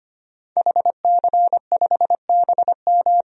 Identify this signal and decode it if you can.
Morse code, heard as a keyed tone, about 25 wpm, 700 Hz. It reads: HC5BM
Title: DXCC-100 Call Signs - 5 Characters - 25wpm